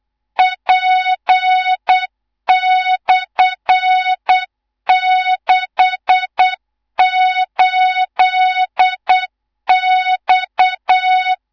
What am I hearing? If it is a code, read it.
Morse code, heard as a keyed tone, about 8 wpm, 3700 Hz, with noise. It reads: P/68X
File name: Morze-HA2MN / hang / 18 IRASJELEK2